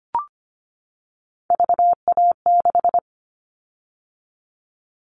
Morse code, heard as a keyed tone, about 25 wpm, 700 Hz. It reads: VA6
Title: Callsign Prefixes for Top 25 DXCC - 25wpm